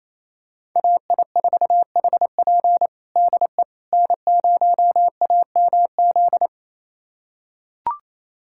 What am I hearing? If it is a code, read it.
Morse code, heard as a keyed tone, about 28 wpm, 700 Hz. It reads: AI4HP DE N0AMZ
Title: USA Callsign DE Callsign - 5 Characters - 28wpm